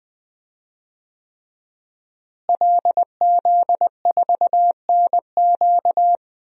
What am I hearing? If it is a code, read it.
Morse code, heard as a keyed tone, about 20 wpm, 700 Hz. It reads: LZ4NQ
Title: DXCC-100 Call Signs - 5 Characters - Repeated 3x - 20wpm